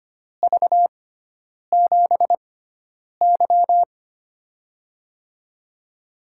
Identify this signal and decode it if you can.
Morse code, heard as a keyed tone, about 25 wpm, spaced as for 9 wpm, 700 Hz. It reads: V7Y
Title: USA Call Signs - 6x Character Spacing - 25wpm